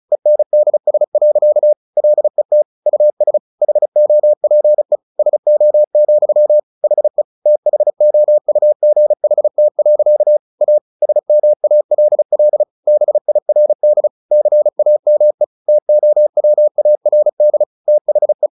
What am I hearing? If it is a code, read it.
Morse code, heard as a keyed tone, about 34 wpm, 600 Hz. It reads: ENDS. LET US HOPE SO, HE THOUGHT. A SMALL BIRD CAME TOWARD THE
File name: the old man and the sea / the old man and the sea 1 35wpm